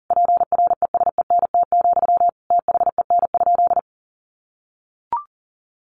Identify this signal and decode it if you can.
Morse code, heard as a keyed tone, about 37 wpm, 700 Hz. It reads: PRESENT, THEN?
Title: Sentences from Top 800 Words 40wpm